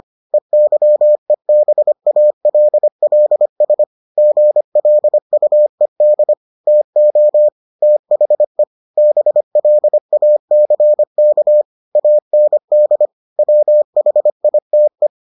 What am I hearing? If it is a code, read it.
Morse code, heard as a keyed tone, about 25 wpm, 600 Hz. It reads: EYEBALLS GLUED TO THE BLACK AND WHITE